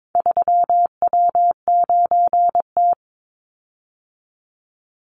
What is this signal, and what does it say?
Morse code, heard as a keyed tone, about 22 wpm, 700 Hz. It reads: 3W9T
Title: Call Signs - 22wpm